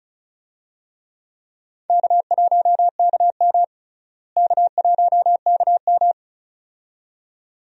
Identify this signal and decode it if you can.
Morse code, heard as a keyed tone, about 36 wpm, 700 Hz. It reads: K1KM K1KM
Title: USA Call Signs - 4 Characters - Repeated 2x - 35wpm